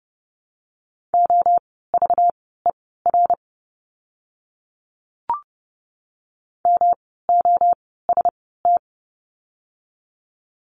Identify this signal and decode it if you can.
Morse code, heard as a keyed tone, about 30 wpm, spaced as for 17 wpm, 700 Hz. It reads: OVER MOST
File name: course / 30:10wpm / 237 Lesson 38f - Letter q - Words 30wpm